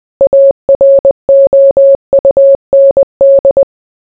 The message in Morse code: AROUND

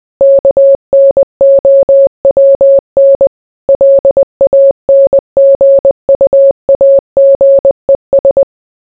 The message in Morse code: KNOWN LANGUAGES

